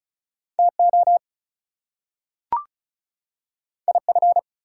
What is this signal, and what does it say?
Morse code, heard as a keyed tone, about 36 wpm, 700 Hz. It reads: TO IF
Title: Two Letter Words - 35wpm